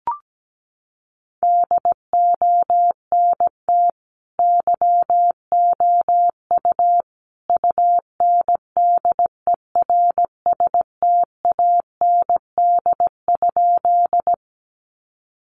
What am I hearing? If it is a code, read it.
Morse code, heard as a keyed tone, about 17 wpm, 700 Hz. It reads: DONT YOU UNDERSTAND?
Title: Sentences from Top 600 Words 17wpm